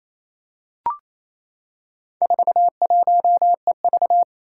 Morse code, heard as a keyed tone, about 28 wpm, 700 Hz. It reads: 41EV
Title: Four Challenging Letter-Number Combinations 28wpm